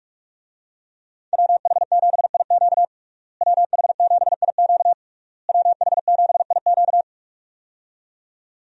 Morse code, heard as a keyed tone, about 46 wpm, 700 Hz. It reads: WH7IQ WH7IQ WH7IQ
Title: USA Call Signs - 5 Characters - Repeated 3x - 45wpm